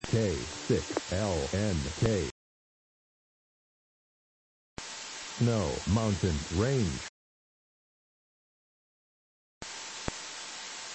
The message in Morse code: none

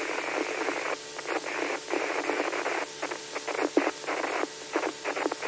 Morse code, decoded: none

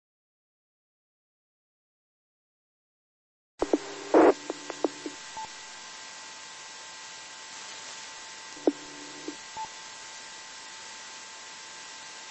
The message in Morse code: none

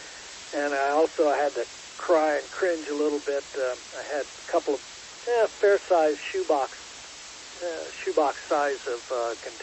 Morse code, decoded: none

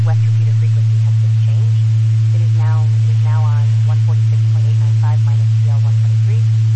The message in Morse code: none

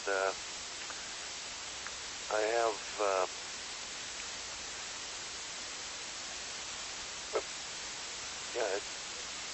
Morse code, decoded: none

